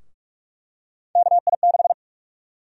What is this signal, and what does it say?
Morse code, heard as a keyed tone, about 46 wpm, 700 Hz. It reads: KI6
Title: USA Call Sign Prefixes - Repeated 2x - 45wpm